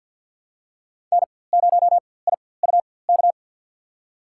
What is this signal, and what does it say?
Morse code, heard as a keyed tone, about 53 wpm, spaced as for 24 wpm, 700 Hz. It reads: N0IUK
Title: USA Call Signs - 4x Character Spacing - 50wpm